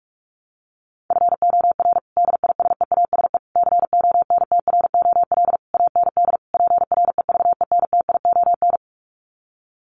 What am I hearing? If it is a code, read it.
Morse code, heard as a keyed tone, about 44 wpm, 700 Hz. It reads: FOR DISEASE CONTROL AND PREVENTION
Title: Sets of 5 Words Encore2 45wpm